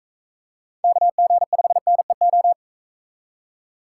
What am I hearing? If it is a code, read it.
Morse code, heard as a keyed tone, about 40 wpm, 700 Hz. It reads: KG5NEO